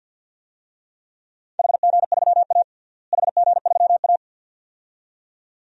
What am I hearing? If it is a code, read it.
Morse code, heard as a keyed tone, about 51 wpm, 700 Hz. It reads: HG3A HG3A
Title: DXCC-100 Call Signs - 4 Characters - Repeated 2x - 50wpm